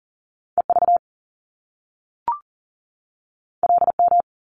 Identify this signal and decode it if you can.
Morse code, heard as a keyed tone, about 39 wpm, 700 Hz. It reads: EV LM
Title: Top Two Letter Combinations in Words 40wpm